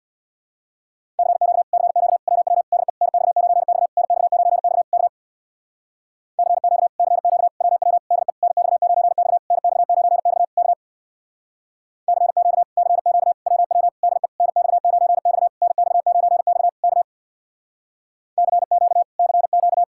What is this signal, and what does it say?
Morse code, heard as a keyed tone, about 66 wpm, 700 Hz. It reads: CQ CQ WW DE NX9X NX9X K CQ CQ WW DE NX9X NX9X K CQ CQ WW DE NX9X NX9X K CQ CQ